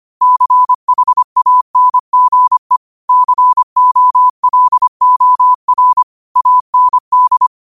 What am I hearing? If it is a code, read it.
Morse code, heard as a keyed tone, about 25 wpm, 1000 Hz. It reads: CHANGE COLOR AND
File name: qotdfiles / 25wpm